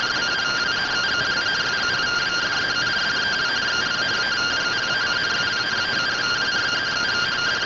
Two-tone teleprinter signal, about 170 Hz shift.
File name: findings / CORONAL fr69 nb202 3mm.gen-44 rtty rx